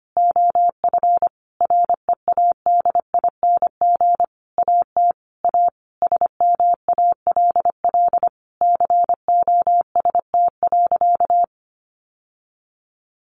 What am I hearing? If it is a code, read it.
Morse code, heard as a keyed tone, about 25 wpm, 700 Hz. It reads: OF READING AT A SMALL COST.
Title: Sentences from Top 900 Words 25wpm